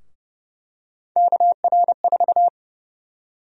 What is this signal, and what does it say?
Morse code, heard as a keyed tone, about 30 wpm, 700 Hz. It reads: KR4
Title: USA Call Sign Prefixes - Repeated 2x - 30wpm